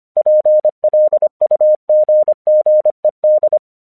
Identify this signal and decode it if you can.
Morse code, heard as a keyed tone, about 25 wpm, 610 Hz. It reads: PLUGGED
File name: 1 English 25wpm / AQC30